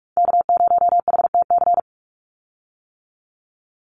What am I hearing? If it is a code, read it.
Morse code, heard as a keyed tone, about 44 wpm, 700 Hz. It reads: K0HTC
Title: USA Call Signs 45wpm